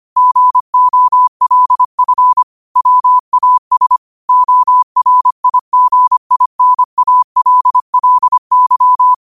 Morse code, heard as a keyed tone, about 25 wpm, 1000 Hz. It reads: GOLF WAS ORIGINALLY
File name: qotdfiles / 25wpm